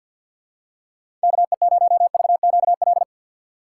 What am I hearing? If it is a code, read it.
Morse code, heard as a keyed tone, about 51 wpm, 700 Hz. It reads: KE0VQL
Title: USA Call Signs - 4x Character Spacing - 50wpm